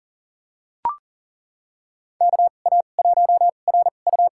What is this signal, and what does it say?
Morse code, heard as a keyed tone, about 40 wpm, spaced as for 29 wpm, 700 Hz. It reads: KA1RU
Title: USA Call Signs - 2x Character Spacing - 40wpm